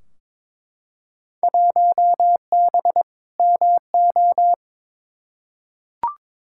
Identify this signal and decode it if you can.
Morse code, heard as a keyed tone, about 22 wpm, 700 Hz. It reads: 1B MO
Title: Field Day Exchanges - 22wpm